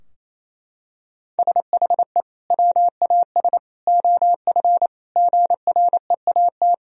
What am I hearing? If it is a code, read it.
Morse code, heard as a keyed tone, about 28 wpm, 700 Hz. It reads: SHE WAS OF GREAT
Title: Sentences from Top 1000 Words - Review Part 4 - 28wpm